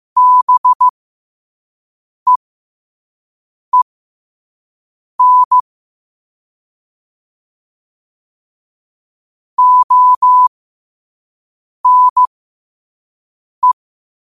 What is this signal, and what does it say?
Morse code, heard as a keyed tone, about 15 wpm, spaced as for 5 wpm, 1000 Hz. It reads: BEEN ONE
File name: 5wpm